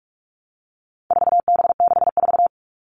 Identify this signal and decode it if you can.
Morse code, heard as a keyed tone, about 44 wpm, 700 Hz. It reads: 4B64
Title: All Cut Numbers - 4-Digit 45wpm